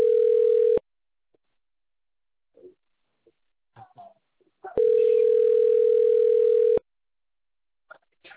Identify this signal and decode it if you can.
Ringback tone (North America), 440 + 480 Hz.